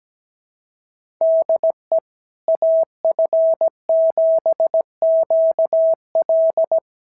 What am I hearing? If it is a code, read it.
Morse code, heard as a keyed tone, about 17 wpm, 650 Hz. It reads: DE AF7QL
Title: SOTA QSO Practice 17wpm